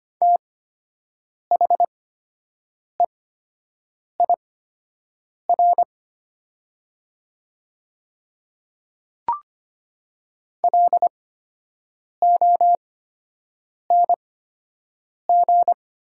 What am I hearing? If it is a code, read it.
Morse code, heard as a keyed tone, about 25 wpm, spaced as for 7 wpm, 700 Hz. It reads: THEIR LONG
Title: 8x Character Spacing - Top 100 Words - 25wpm